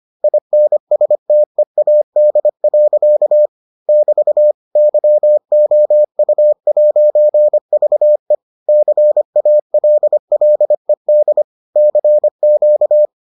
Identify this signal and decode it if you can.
Morse code, heard as a keyed tone, about 25 wpm, 600 Hz. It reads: INSTEAD. <BT> YOU'VE CALLED CQ